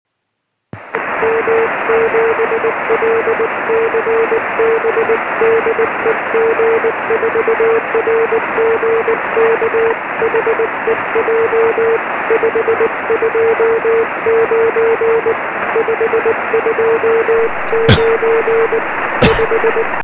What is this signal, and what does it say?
Morse code, heard as a keyed tone, about 19 wpm, 450 Hz, with noise. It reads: M7LCB DE G4RGK HEJ 529 529 5
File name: sm7lcb / g4rgk 23cm